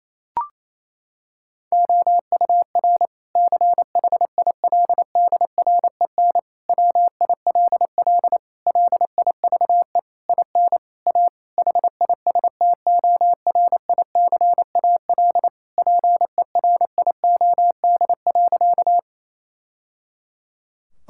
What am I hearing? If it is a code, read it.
Morse code, heard as a keyed tone, about 28 wpm, 700 Hz. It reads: OUR CHILDREN WILL LIVE IN A HISTORICAL PERIOD.
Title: Sentences from Top 1000 Words - Review Part 5 - 28wpm